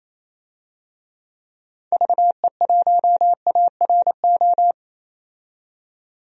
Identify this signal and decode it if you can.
Morse code, heard as a keyed tone, about 27 wpm, 700 Hz. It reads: VE1ARO
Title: DXCC-100 Call Signs - 6 Characters - 28wpm